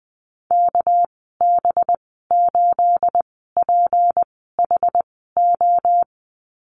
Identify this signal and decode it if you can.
Morse code, heard as a keyed tone, about 20 wpm, spaced as for 14 wpm, 700 Hz. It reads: KB8PHO
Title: 213 Lesson 34g - Number 8 - Call Signs 20wpm